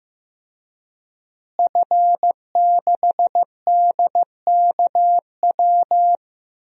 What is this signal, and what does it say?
Morse code, heard as a keyed tone, about 15 wpm, 700 Hz. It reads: F6DKW
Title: DXCC-100 Call Signs - 5 Characters - Repeated 3x - 15wpm